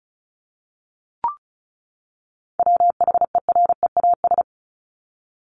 Morse code, heard as a keyed tone, about 34 wpm, 700 Hz. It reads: WHEREAS
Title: Top 1501-1750 Words 35wpm